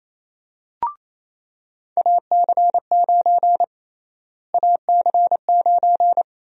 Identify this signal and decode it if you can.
Morse code, heard as a keyed tone, about 28 wpm, 700 Hz. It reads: AC9 AC9
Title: USA Call Sign Prefixes - Repeated 2x - 28wpm